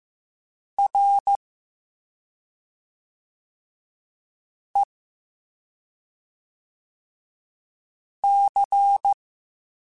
Morse code, heard as a keyed tone, about 15 wpm, spaced as for 2 wpm, 800 Hz. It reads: REC